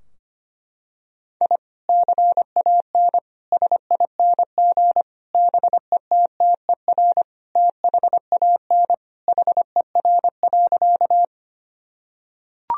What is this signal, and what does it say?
Morse code, heard as a keyed tone, about 25 wpm, 700 Hz. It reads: I CAN SING BETTER THAN HER.
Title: Sentences from Top 1000 Words - Review Part 2 - 25wpm